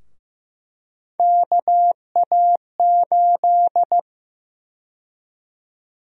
Morse code, heard as a keyed tone, about 15 wpm, 700 Hz. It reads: KA8
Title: USA Call Sign Prefixes - Repeated 2x - 15wpm